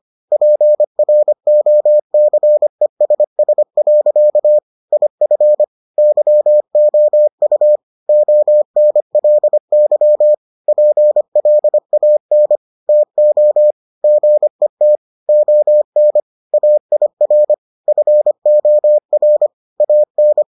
Morse code, heard as a keyed tone, about 25 wpm, 600 Hz. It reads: PROCESS. IF YOU ONLY PLAN TO GET ON AIR FOR AN